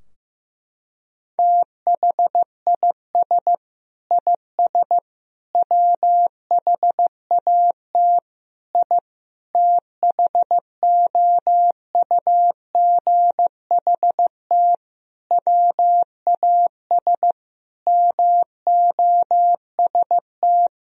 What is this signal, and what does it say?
Morse code, heard as a keyed tone, about 15 wpm, 700 Hz. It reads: THIS IS WHAT I THOUGHT WAS MOST
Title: Sentences from Top 1000 Words - Review Part 4 - 15wpm